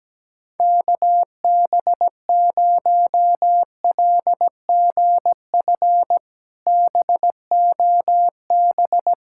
Morse code, heard as a keyed tone, about 17 wpm, 700 Hz. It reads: KB0LGF BOB
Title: K1USN SST QSO Practice - 17wpm